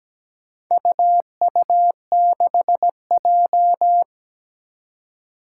Morse code, heard as a keyed tone, about 17 wpm, 700 Hz. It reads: UU6J
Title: Four Challenging Letter-Number Combinations 17wpm